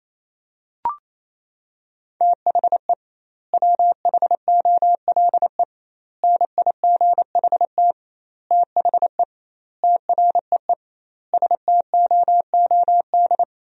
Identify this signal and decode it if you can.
Morse code, heard as a keyed tone, about 28 wpm, 700 Hz. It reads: THE WHOLE NIGHT THE TREE STOOD
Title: 2x Word-Spacing - Sentences from Top 500 Words - 28wpm